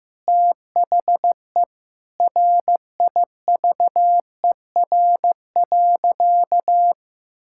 Morse code, heard as a keyed tone, about 15 wpm, 700 Hz. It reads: THE RIVER.
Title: Sentences from Top 1000 Words - Review Part 5 - 15wpm